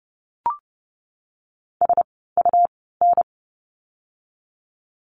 Morse code, heard as a keyed tone, about 30 wpm, spaced as for 17 wpm, 700 Hz. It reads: SUN
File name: Lesson 17f - Letter u - Words 30wpm